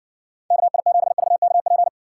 Morse code, heard as a keyed tone, about 65 wpm, 700 Hz. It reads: KI7VKP